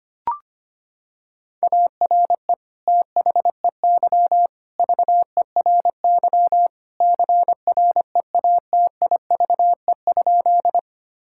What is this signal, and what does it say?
Morse code, heard as a keyed tone, about 25 wpm, 700 Hz. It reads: ARE THEY VERY CREATIVE?
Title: Sentences from Top 1000 Words - Review Part 5 - 25wpm